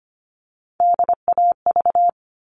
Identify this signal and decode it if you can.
Morse code, heard as a keyed tone, about 25 wpm, 700 Hz. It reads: DAV
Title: All Cut Numbers - 3-Digit 25wpm